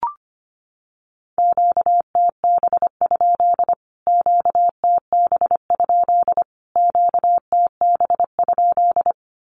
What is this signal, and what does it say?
Morse code, heard as a keyed tone, about 25 wpm, 700 Hz. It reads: QTB? QTB? QTB?
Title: Q-Codes Repeated 3x 25wpm